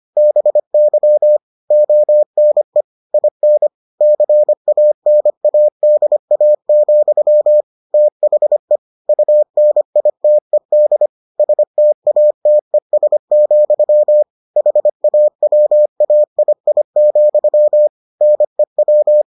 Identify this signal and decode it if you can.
Morse code, heard as a keyed tone, about 25 wpm, 600 Hz. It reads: BY ONE IN CANADA, THE UNITED STATES, HAWAII, NEW